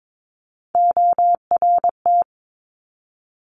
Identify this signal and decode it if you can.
Morse code, heard as a keyed tone, about 22 wpm, 700 Hz. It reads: ORT
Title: Top Three Letter Combinations in Words 22wpm